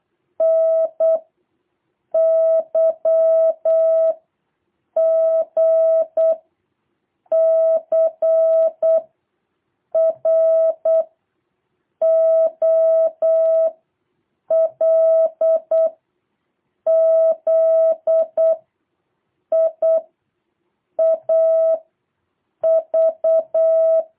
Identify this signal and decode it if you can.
Morse code, heard as a keyed tone, about 8 wpm, 660 Hz. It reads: N Y G C R O L Z I A V
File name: jh8cbh / 2023cw